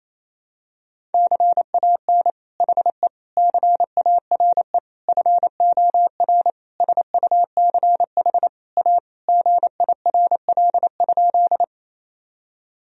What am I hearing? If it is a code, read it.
Morse code, heard as a keyed tone, about 28 wpm, 700 Hz. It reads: CAN HE CARE FOR SUCH A GIRL?
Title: Sentences from Top 1000 Words - Review Part 1 - 28wpm